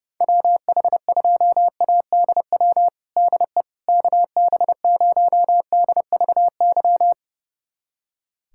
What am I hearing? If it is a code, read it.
Morse code, heard as a keyed tone, about 30 wpm, 700 Hz. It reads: WH2ADW DE KB0DVY
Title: USA Callsign DE Callsign - 6 Characters - 30wpm